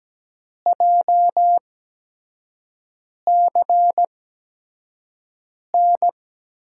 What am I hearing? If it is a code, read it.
Morse code, heard as a keyed tone, about 17 wpm, spaced as for 5 wpm, 700 Hz. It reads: JCN
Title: USA Call Signs - 8x Character Spacing - 17wpm